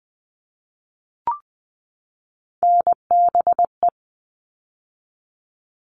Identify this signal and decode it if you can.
Morse code, heard as a keyed tone, about 20 wpm, 700 Hz. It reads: NBE
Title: All Cut Numbers - 3-Digit 20wpm